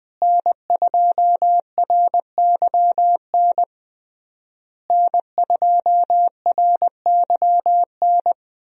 Morse code, heard as a keyed tone, about 19 wpm, 700 Hz. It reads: N2RYN N2RYN
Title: USA Call Signs - 5 Characters - Repeated 3x - 20wpm